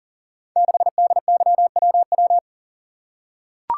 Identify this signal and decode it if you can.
Morse code, heard as a keyed tone, about 41 wpm, 700 Hz. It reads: 6DYWW
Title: Code Groups - 5 Letter:Numbers - 40wpm